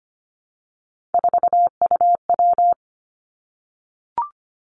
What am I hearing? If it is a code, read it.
Morse code, heard as a keyed tone, about 25 wpm, 700 Hz. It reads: 4UW
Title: Three Challenging Letter-Number Combinations 25wpm